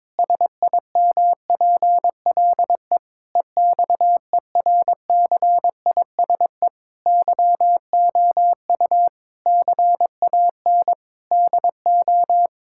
Morse code, heard as a keyed tone, about 21 wpm, 700 Hz. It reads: SIMPLE EXERCISE YOU CAN DO
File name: Difficult Sentences with Hints - 22wpm